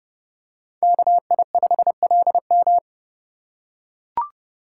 Morse code, heard as a keyed tone, about 29 wpm, 700 Hz. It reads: KI5LM